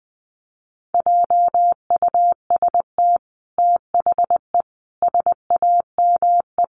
Morse code, heard as a keyed tone, about 20 wpm, 700 Hz. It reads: JUST THE SAME
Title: Sentences from Top 900 Words 20wpm